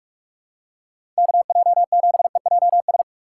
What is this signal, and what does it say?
Morse code, heard as a keyed tone, about 43 wpm, 700 Hz. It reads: KJ7EJS
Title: USA Call Signs - 6 Characters - 6x Character Spacing - 45wpm